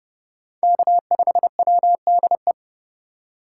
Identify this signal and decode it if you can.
Morse code, heard as a keyed tone, about 28 wpm, 700 Hz. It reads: K5WDE